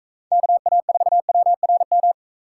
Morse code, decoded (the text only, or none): KA4WRM